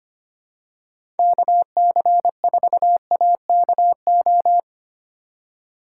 Morse code, heard as a keyed tone, about 25 wpm, 700 Hz. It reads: KC4AKO
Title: USA Call Signs - 6 Characters - 2x Character Spacing - 25wpm